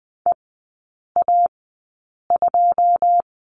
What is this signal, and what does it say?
Morse code, heard as a keyed tone, about 20 wpm, spaced as for 8 wpm, 700 Hz. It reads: EA2
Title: 133 Lesson 22e - Question Mark - Three Characters 20wpm